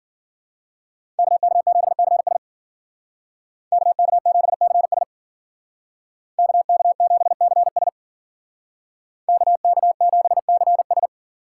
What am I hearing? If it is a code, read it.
Morse code, heard as a keyed tone, about 49 wpm, 700 Hz. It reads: KK7CS KK7CS KK7CS KK7CS
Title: Speed Racing - USA Call Signs 40wpm